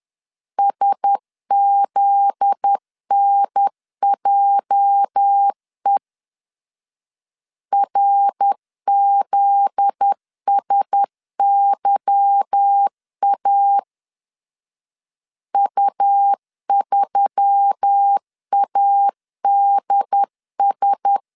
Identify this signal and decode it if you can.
Morse code, heard as a keyed tone, about 10 wpm, 800 Hz. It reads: SZNJE RZSYA U3ADS
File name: LEZIONI / 8 YJ45 / 8 YJ 10wpm 2651